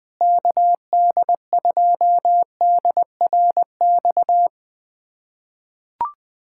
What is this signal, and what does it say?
Morse code, heard as a keyed tone, about 21 wpm, 700 Hz. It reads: KD2DRX
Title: USA Callsign DE Callsign - 6 Characters - 20wpm